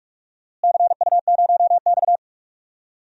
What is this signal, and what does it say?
Morse code, heard as a keyed tone, about 43 wpm, 700 Hz. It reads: CU0X
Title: DXCC-100 Call Signs - 4 Characters - Repeated 2x - 45wpm